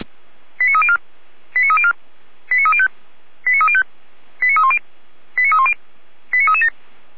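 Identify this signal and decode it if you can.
Five-tone selective call (ZVEI), heard as quick runs of five tones, tones of about 80 ms. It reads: 89304 89304 89305 89305 89310 89310 89337
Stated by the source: ZVEI five-tone selective call, code 89246, received with an SDR.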